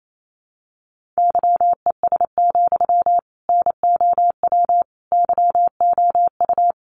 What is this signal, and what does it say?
Morse code, heard as a keyed tone, about 28 wpm, 700 Hz. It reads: YES, NOW YOU
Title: Sentences from Top 800 Words 28wpm